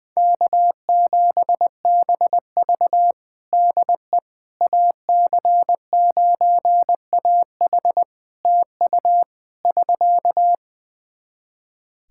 Morse code, heard as a keyed tone, about 20 wpm, 700 Hz. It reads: K7BV DE AC9AH TU <SK>
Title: Practice QSO - Exchange 4 - Ending - 20wpm